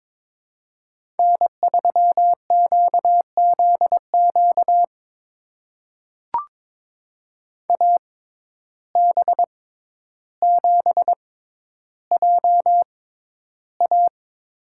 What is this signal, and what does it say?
Morse code, heard as a keyed tone, about 22 wpm, spaced as for 11 wpm, 700 Hz. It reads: N3QZQ AB7JA
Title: USA Call Signs - 5 Characters - 6x Character Spacing - 22wpm